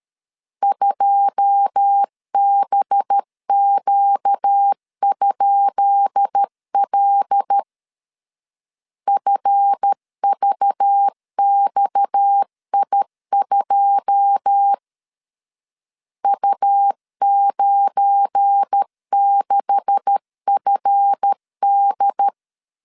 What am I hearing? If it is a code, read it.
Morse code, heard as a keyed tone, about 13 wpm, 800 Hz. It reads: 2BQ?L FVXI2 U96FD
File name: LEZIONI / 10 SIGNS / 10B INT 12wpm 2414